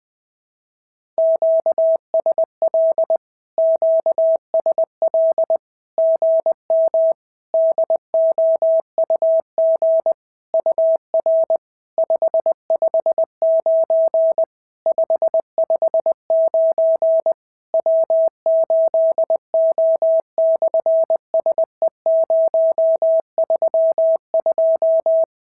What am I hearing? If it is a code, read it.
Morse code, heard as a keyed tone, about 19 wpm, 650 Hz. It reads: QSL QSL GM DOUG UR 559 559 W8O/SE032
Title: SOTA QSO Practice 20wpm